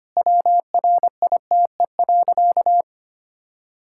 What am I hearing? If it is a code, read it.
Morse code, heard as a keyed tone, about 25 wpm, 700 Hz. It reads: WRITE.
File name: Easy Sentences with Hints - 25wpm